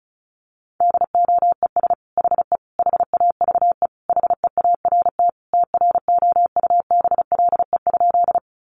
Morse code, heard as a keyed tone, about 34 wpm, 700 Hz. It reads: DOES HE HAVE HEART TROUBLE?
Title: Sentences from Top 1200 Words 35wpm